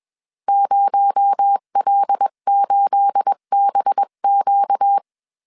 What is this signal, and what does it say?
Morse code, heard as a keyed tone, about 21 wpm, 800 Hz. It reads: 0L8BQ